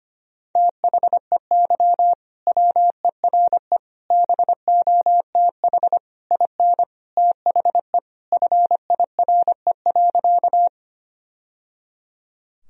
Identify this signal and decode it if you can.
Morse code, heard as a keyed tone, about 25 wpm, 700 Hz. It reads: THEY WERE BOTH IN THE FIRE.
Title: Sentences from Top 1000 Words - Review Part 1 - 25wpm